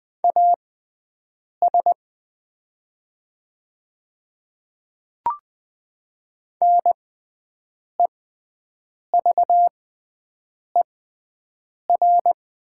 Morse code, heard as a keyed tone, about 20 wpm, spaced as for 7 wpm, 700 Hz. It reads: AS NEVER
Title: 6x Character Spacing - Top 300 Words - 20wpm